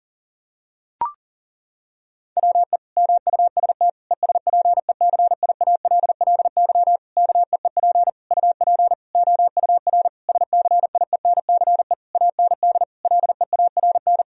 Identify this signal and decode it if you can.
Morse code, heard as a keyed tone, about 40 wpm, 700 Hz. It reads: WE MUST ESPECIALLY KEEP UP OUR SCIENCE AND LEARN
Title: Sentences from Top 1000 Words - Review Part 4 - 40wpm